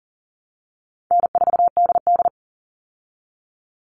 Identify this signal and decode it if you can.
Morse code, heard as a keyed tone, about 40 wpm, 700 Hz. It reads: N4DD
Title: Call Signs Repeated 3x - 40wpm